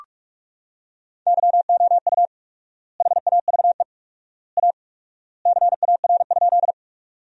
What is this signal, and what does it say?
Morse code, heard as a keyed tone, about 41 wpm, 700 Hz. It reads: YOU HAVE A CAR?
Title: 4x Word-Spacing - Sentences from Top 300 Words - 45wpm